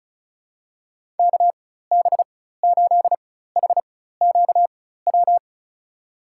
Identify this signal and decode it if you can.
Morse code, heard as a keyed tone, about 34 wpm, spaced as for 16 wpm, 700 Hz. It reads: KB8HQW
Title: USA Call Signs - 6 Characters - 4x Character Spacing - 35wpm